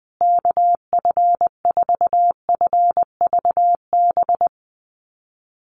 Morse code, heard as a keyed tone, about 20 wpm, 700 Hz. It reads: KF4FVB
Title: USA Call Signs Repeated 3x 20wpm